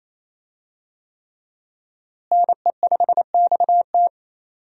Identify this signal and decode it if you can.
Morse code, heard as a keyed tone, about 28 wpm, 700 Hz. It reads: NE5XT
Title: USA Call Signs - 5 Characters - Repeated 3x - 28wpm